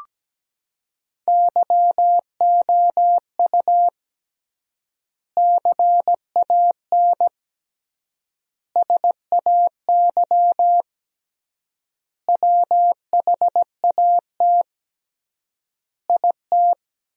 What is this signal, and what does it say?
Morse code, heard as a keyed tone, about 17 wpm, 700 Hz. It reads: YOU CAN SAY WHAT IT
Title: 3x Word-Spacing - Sentences from Top 200 Words - 17wpm